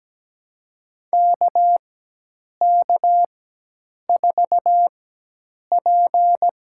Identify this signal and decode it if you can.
Morse code, heard as a keyed tone, about 17 wpm, spaced as for 8 wpm, 700 Hz. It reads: KK4P
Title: USA Call Signs - 4x Character Spacing - 17wpm